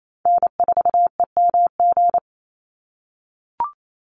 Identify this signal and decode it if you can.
Morse code, heard as a keyed tone, about 28 wpm, 700 Hz. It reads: N4EMG
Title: NA SOTA Chaser Callsigns 28wpm